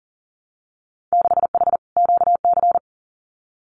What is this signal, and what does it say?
Morse code, heard as a keyed tone, about 40 wpm, 700 Hz. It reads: 6H QC
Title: Winter Field Day Exchanges Encore 40wpm